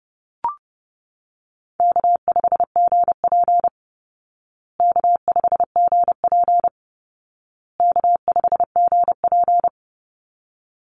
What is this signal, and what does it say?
Morse code, heard as a keyed tone, about 30 wpm, 700 Hz. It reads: K5GP K5GP K5GP
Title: Call Signs Repeated 2x - 30wpm